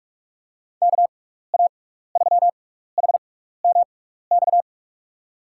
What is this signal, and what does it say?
Morse code, heard as a keyed tone, about 44 wpm, 700 Hz. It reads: K A 3 H M X
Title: USA Call Signs - 6 Characters - 6x Character Spacing - 45wpm